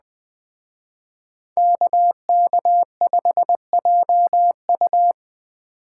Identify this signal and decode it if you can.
Morse code, heard as a keyed tone, about 20 wpm, 700 Hz. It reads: KK5JU
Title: Speed Racing - USA 2x2 Call Signs 17wpm